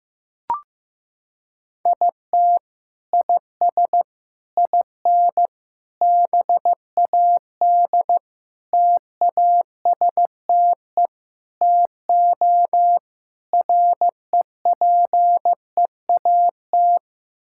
Morse code, heard as a keyed tone, about 15 wpm, 700 Hz. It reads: IT IS IN BAD TASTE TO REPEAT
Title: Sentences from Top 1600 Words 15wpm